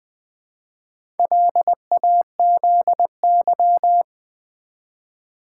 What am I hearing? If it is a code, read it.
Morse code, heard as a keyed tone, about 20 wpm, 700 Hz. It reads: LAZY
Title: Build A Word - 1 Letter At A Time - 4-Letter Words - 20wpm